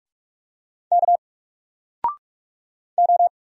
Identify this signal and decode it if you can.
Morse code, heard as a keyed tone, about 46 wpm, 700 Hz. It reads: K O
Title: Single Letter-Number Alt 45wpm